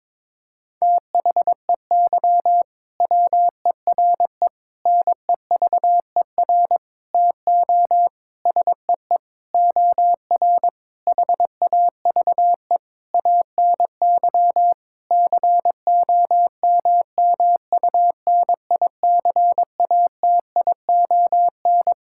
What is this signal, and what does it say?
Morse code, heard as a keyed tone, about 22 wpm, 700 Hz. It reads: THEY WERE NEVER TO SEE OR HAVE ANY COMMUNICATION
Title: Sentences from Top 1000 Words - Review Part 6 - 22wpm